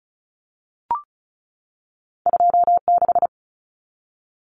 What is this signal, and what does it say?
Morse code, heard as a keyed tone, about 36 wpm, 700 Hz. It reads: 26
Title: Numbers - 2-Digit 35wpm